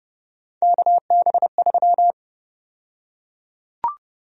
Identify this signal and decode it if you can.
Morse code, heard as a keyed tone, about 32 wpm, 700 Hz. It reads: KB3
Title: USA Call Sign Prefixes - Repeated 2x - 30wpm